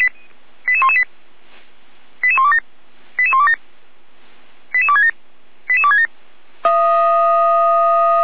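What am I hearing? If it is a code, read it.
Five-tone selective call (ZVEI), heard as quick runs of five tones, tones of about 70 ms. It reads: 88118 88127 88127 88267 88267
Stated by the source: ZVEI five-tone call 88191 received by SDR.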